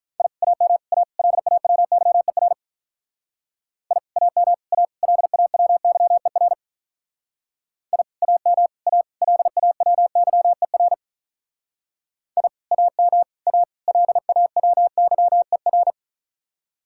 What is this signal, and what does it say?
Morse code, heard as a keyed tone, about 45 wpm, 700 Hz. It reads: I AM A LAWYER I AM A LAWYER I AM A LAWYER I AM A LAWYER